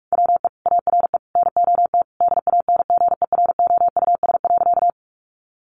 Morse code, heard as a keyed tone, about 44 wpm, 700 Hz. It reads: WE ARE NOT DANGEROUS.
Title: Sentences from Top 1200 Words 45wpm